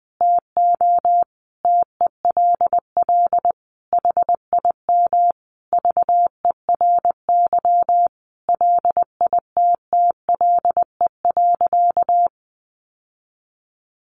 Morse code, heard as a keyed tone, about 20 wpm, 700 Hz. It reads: TO TELL HIM VERY LITTLE.
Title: Sentences from Top 800 Words 20wpm